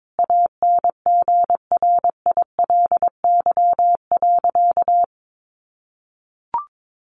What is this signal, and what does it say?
Morse code, heard as a keyed tone, about 22 wpm, 700 Hz. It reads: ANGRILY.
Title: A Sound Of Thunder 22wpm